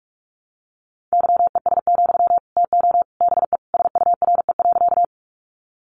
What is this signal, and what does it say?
Morse code, heard as a keyed tone, about 44 wpm, 700 Hz. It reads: YES, TO BE SURE.
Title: Sentences from Top 500 Words 45wpm